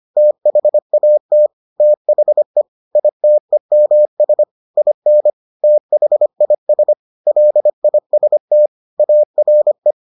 Morse code, heard as a keyed tone, about 25 wpm, 600 Hz. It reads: THAT THE ITEMS IN THIS LIST ARE